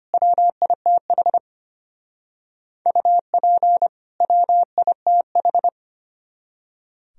Morse code, heard as a keyed tone, about 32 wpm, 700 Hz. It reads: WITH UP WITH